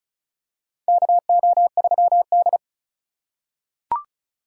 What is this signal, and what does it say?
Morse code, heard as a keyed tone, about 36 wpm, 700 Hz. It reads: KO3D
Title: USA Call Signs - 3x Character Spacing - 35wpm